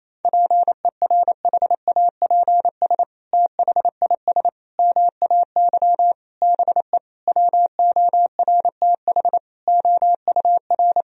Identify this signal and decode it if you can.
Morse code, heard as a keyed tone, about 28 wpm, 690 Hz. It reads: PERHAPS THIS MAY BE WORTH OUR
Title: Sentences from Top 1000 Words - Review Part 2 - 28wpm